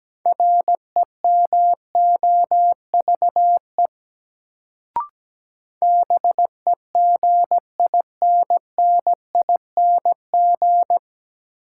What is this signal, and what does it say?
Morse code, heard as a keyed tone, about 17 wpm, 700 Hz. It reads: REMOVE BEGINNING
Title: Sentences from Top 800 Words - New Word Drill - 17wpm